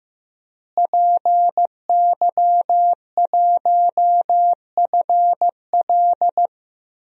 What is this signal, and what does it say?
Morse code, heard as a keyed tone, about 15 wpm, 700 Hz. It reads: PY1FL
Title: DXCC-100 Call Signs - 5 Characters - Repeated 2x - 15wpm